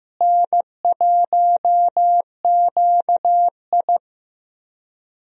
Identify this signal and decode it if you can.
Morse code, heard as a keyed tone, about 15 wpm, 700 Hz. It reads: N1QI
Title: USA Call Signs - 4 Characters - Repeated 3x - 15wpm